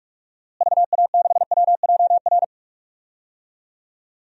Morse code, heard as a keyed tone, about 46 wpm, 700 Hz. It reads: VA6WJR
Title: DXCC-100 Call Signs - 6 Characters - Repeated 2x - 45wpm